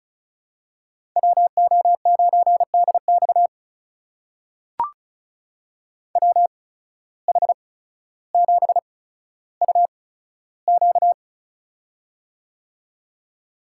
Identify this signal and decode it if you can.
Morse code, heard as a keyed tone, about 36 wpm, spaced as for 15 wpm, 700 Hz. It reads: WO9DX WH7UQ
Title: USA Call Signs - 5 Characters - 8x Character Spacing - 35wpm